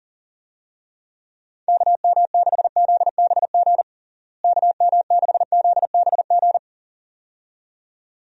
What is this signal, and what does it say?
Morse code, heard as a keyed tone, about 41 wpm, 700 Hz. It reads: KM6ZBG KM6ZBG
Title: USA Call Signs - 6 Characters - Repeated 2x - 40wpm